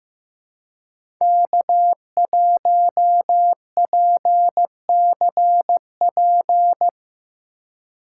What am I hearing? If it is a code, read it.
Morse code, heard as a keyed tone, about 17 wpm, 700 Hz. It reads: K1PCP
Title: Speed Racing - USA Call Signs 15wpm